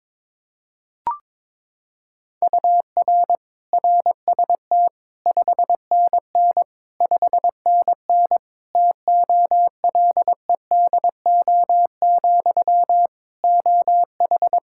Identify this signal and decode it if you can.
Morse code, heard as a keyed tone, about 22 wpm, 700 Hz. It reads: UR RST 5NN 5NN TOLEDO, OH
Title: Practice QSO - Exchange 1 - Beginning - 22wpm